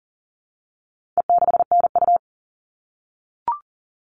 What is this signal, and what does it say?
Morse code, heard as a keyed tone, about 40 wpm, 700 Hz. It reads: E6NU
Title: All Cut Numbers - 4-Digit 40wpm